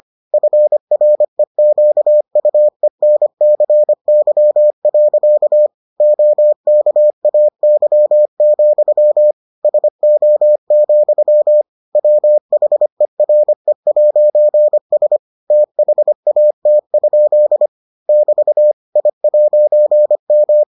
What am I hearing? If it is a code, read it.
Morse code, heard as a keyed tone, about 25 wpm, 600 Hz. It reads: FREQUENCY. OKAY, SO, WHERE'S THAT? <BT> I'M